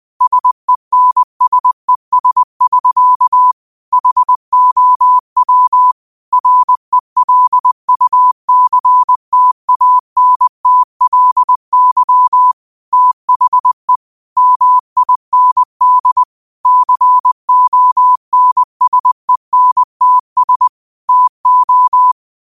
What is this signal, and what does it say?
Morse code, heard as a keyed tone, about 20 wpm, 1000 Hz. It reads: SENSES <SK> HOW RELUCTANTLY THE MIND CONSENTS TO
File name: qotdfiles / 20wpm